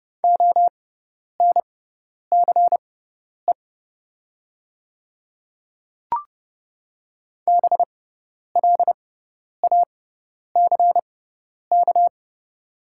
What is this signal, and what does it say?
Morse code, heard as a keyed tone, about 30 wpm, spaced as for 10 wpm, 700 Hz. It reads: ONCE BLACK
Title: 6x Character Spacing - Top 500 Words - 30wpm